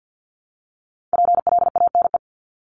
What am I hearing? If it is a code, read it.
Morse code, heard as a keyed tone, about 50 wpm, 700 Hz. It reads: PLANE
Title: Top 500 Words Repeated 50wpm